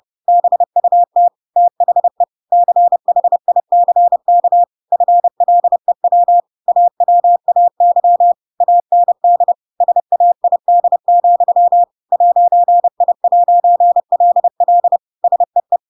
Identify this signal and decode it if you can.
Morse code, heard as a keyed tone, about 30 wpm, 700 Hz. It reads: BUT THE CHICK FLEW AWAY AND SAID, 'I'LL SEE
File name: Pinocchio / Pinocchio ch7 30 30 ik1hge